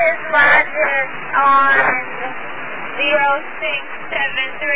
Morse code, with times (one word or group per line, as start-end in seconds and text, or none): none